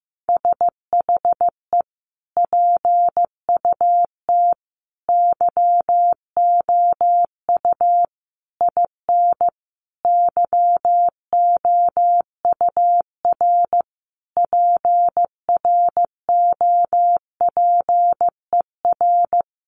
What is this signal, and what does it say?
Morse code, heard as a keyed tone, about 15 wpm, 700 Hz. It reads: SHE PUT YOU IN YOUR PROPER
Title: Sentences from Top 800 Words 15wpm